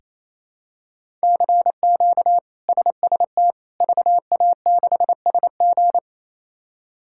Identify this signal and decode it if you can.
Morse code, heard as a keyed tone, about 28 wpm, 700 Hz. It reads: CQ SST VA6SG
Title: K1USN SST QSO Practice - 28wpm